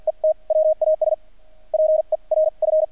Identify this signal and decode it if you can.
Morse code, heard as a keyed tone, about 48 wpm, spaced as for 39 wpm, 630 Hz. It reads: ET WAS PERF
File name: archived / CRSnet